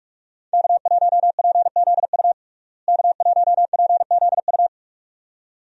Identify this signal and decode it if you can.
Morse code, heard as a keyed tone, about 46 wpm, 700 Hz. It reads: K1PZU K1PZU
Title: USA Call Signs - 5 Characters - Repeated 2x - 45wpm